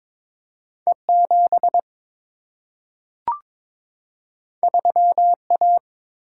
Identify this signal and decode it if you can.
Morse code, heard as a keyed tone, about 22 wpm, 700 Hz. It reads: E7 3A
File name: practice / 22wpm / Two Letter-Number Combinations - 22wpm